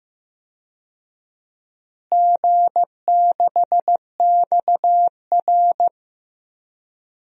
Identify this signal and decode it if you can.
Morse code, heard as a keyed tone, about 15 wpm, 700 Hz. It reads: G6XR
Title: DXCC-100 Call Signs - 4 Characters - Repeated 3x - 15wpm